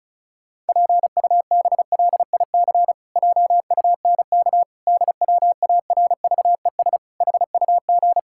Morse code, heard as a keyed tone, about 34 wpm, 700 Hz. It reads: PUBLIC JUNK DWARVES HUG
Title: Pangram Challenge v2 - 35wpm